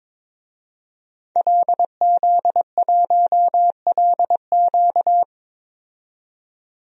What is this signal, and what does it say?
Morse code, heard as a keyed tone, about 22 wpm, 700 Hz. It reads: LZ1LQ
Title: DXCC-100 Call Signs - 5 Characters - 22wpm